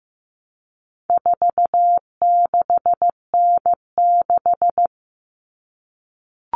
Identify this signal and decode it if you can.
Morse code, heard as a keyed tone, about 15 wpm, 700 Hz. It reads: 46N6
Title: Common Cut Numbers - 4-Digit 15wpm